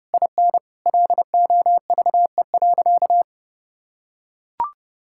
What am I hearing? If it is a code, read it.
Morse code, heard as a keyed tone, about 30 wpm, 700 Hz. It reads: IN LOVE.
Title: Sentences from Top 1600 Words 30wpm